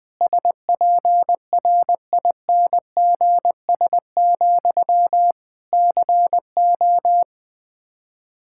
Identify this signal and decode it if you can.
Morse code, heard as a keyed tone, about 21 wpm, 700 Hz. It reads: SPRINGS, CO
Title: Practice QSO - Exchange 1 - Beginning - 20wpm